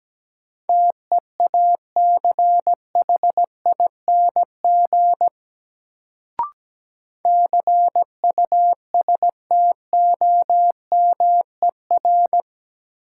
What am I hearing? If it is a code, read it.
Morse code, heard as a keyed tone, about 17 wpm, 700 Hz. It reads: TEACHING CUSTOMER
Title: Sentences from Top 1100 Words - New Word Drill Alt - 17wpm